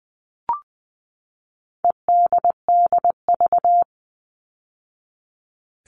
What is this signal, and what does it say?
Morse code, heard as a keyed tone, about 20 wpm, 700 Hz. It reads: EDDV
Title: All Cut Numbers - 4-Digit 20wpm